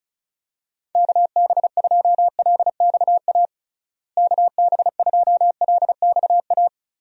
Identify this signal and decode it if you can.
Morse code, heard as a keyed tone, about 36 wpm, 700 Hz. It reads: KB2LXA KB2LXA
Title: USA Call Signs - 6 Characters - Repeated 3x - 35wpm